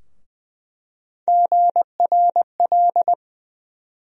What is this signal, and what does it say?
Morse code, heard as a keyed tone, about 20 wpm, 700 Hz. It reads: GRL
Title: USA Call Sign Postfixes - 3 Letters 20wpm